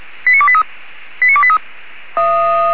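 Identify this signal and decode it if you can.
Five-tone selective call (ZVEI), heard as quick runs of five tones, tones of about 70 ms. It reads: 89283 89283
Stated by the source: ZVEI five-tone call received by SDR.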